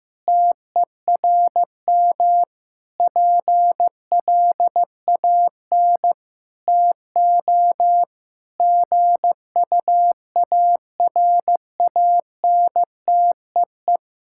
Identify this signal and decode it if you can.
Morse code, heard as a keyed tone, about 15 wpm, 700 Hz. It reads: TERM PLAN TO GUARANTEE
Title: Sentences from Top 1600 Words 15wpm